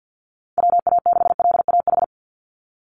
Morse code, heard as a keyed tone, about 48 wpm, 700 Hz. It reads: WA6LAH
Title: USA Call Signs 50wpm